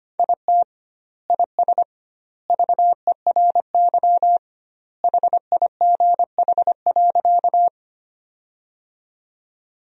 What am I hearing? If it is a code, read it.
Morse code, heard as a keyed tone, about 25 wpm, 700 Hz. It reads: IT IS VERY HIGH.
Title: 2x Word-Spacing - Sentences from Top 200 Words - 25wpm